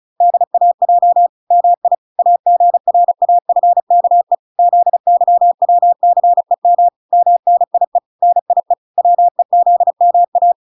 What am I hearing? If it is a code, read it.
Morse code, heard as a keyed tone, about 35 wpm, 700 Hz. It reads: DAJ MI AGRAFKE ZYWCEM MNIE NIE WEZMA